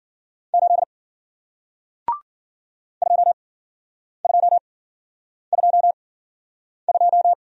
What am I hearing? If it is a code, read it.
Morse code, heard as a keyed tone, about 53 wpm, spaced as for 9 wpm, 700 Hz. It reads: 8 2222